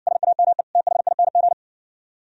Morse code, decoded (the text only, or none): SAME THING